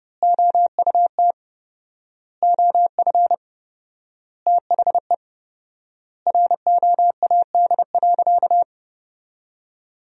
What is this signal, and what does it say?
Morse code, heard as a keyed tone, about 32 wpm, 700 Hz. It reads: OUT OF THE ROAD.
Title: 4x Word-Spacing - Sentences from Top 400 Words - 30wpm